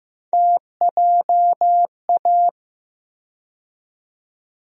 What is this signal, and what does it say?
Morse code, heard as a keyed tone, about 15 wpm, 700 Hz. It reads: TJA
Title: USA Call Sign Postfixes - 3 Letters 15wpm